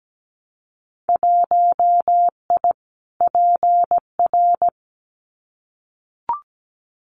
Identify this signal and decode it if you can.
Morse code, heard as a keyed tone, about 17 wpm, 700 Hz. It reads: 1I PR
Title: Winter Field Day Exchanges Encore 17wpm